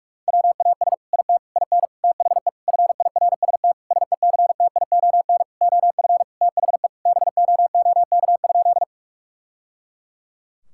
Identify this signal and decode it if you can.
Morse code, heard as a keyed tone, about 41 wpm, 700 Hz. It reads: WAS IT IN THE FIRST SECTION OF THE BOOK?
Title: Sentences from Top 1000 Words - Review Part 5 - 45wpm